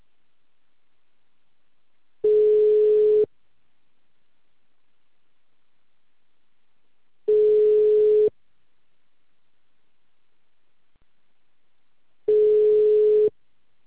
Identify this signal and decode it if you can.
Ringback tone (Europe, 425 Hz).
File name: out- -2044- - - .14200